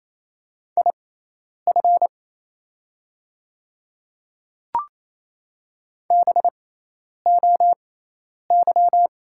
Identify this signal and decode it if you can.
Morse code, heard as a keyed tone, about 27 wpm, spaced as for 10 wpm, 700 Hz. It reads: IF BOY
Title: 6x Character Spacing - Top 500 Words - 28wpm